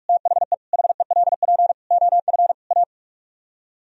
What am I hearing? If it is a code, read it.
Morse code, heard as a keyed tone, about 44 wpm, 700 Hz. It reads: THE HELP OF A